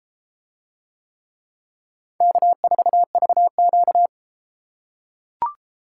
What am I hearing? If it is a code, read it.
Morse code, heard as a keyed tone, about 31 wpm, 700 Hz. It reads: K4VQ